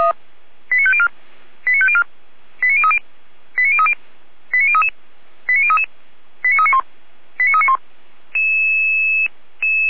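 Five-tone selective call (ZVEI), heard as quick runs of five tones, tones of about 70 ms. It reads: 89504 89504 89030 89030 89033 89033 89391 89391
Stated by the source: ZVEI five-tone selective call received with an SDR.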